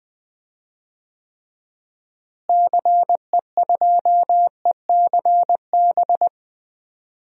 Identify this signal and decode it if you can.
Morse code, heard as a keyed tone, about 20 wpm, 700 Hz. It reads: CE2ECB
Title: DXCC-100 Call Signs - 6 Characters - Repeated 3x - 20wpm